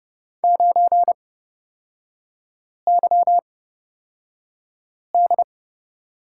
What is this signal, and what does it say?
Morse code, heard as a keyed tone, about 30 wpm, spaced as for 5 wpm, 700 Hz. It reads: 9YD
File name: Lesson 36e - BT - Three Characters 30wpm